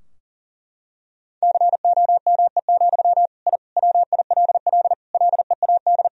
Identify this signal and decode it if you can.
Morse code, heard as a keyed tone, about 40 wpm, 700 Hz. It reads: COME, I WILL LEAD
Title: Sentences from Top 1000 Words - Review Part 2 - 40wpm